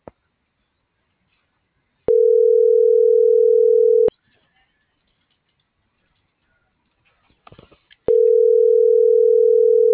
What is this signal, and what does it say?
Ringback tone (North America), 440 + 480 Hz.